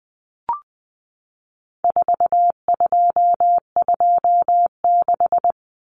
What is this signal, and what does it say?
Morse code, heard as a keyed tone, about 20 wpm, 700 Hz. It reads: 4226
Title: Common Cut Numbers - 4-Digit 20wpm